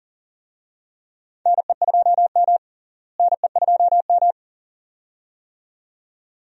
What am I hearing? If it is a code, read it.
Morse code, heard as a keyed tone, about 39 wpm, 700 Hz. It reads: NE2M NE2M
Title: USA Call Signs - 4 Characters - Repeated 2x - 40wpm